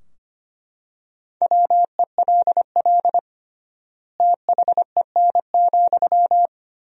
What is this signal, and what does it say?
Morse code, heard as a keyed tone, about 25 wpm, 700 Hz. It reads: WELL THEN,
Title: 3x Word-Spacing - Sentences from Top 500 Words - 25wpm